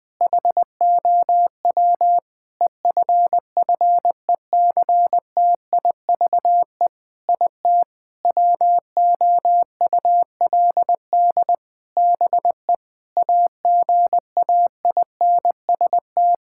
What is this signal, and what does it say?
Morse code, heard as a keyed tone, about 20 wpm, 700 Hz. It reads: HOW EFFECTIVE IT WOULD BE AGAINST
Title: Sentences from Top 1000 Words - Review Part 6 - 20wpm